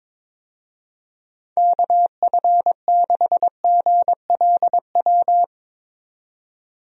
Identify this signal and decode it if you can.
Morse code, heard as a keyed tone, about 22 wpm, 700 Hz. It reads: KF6GLW
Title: USA Call Signs - 6 Characters - 22wpm